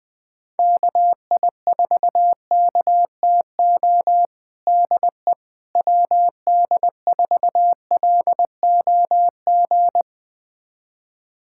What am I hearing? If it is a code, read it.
Morse code, heard as a keyed tone, about 19 wpm, 700 Hz. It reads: KI4KTO DE WD4LOG
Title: USA Callsign DE Callsign - 6 Characters - 20wpm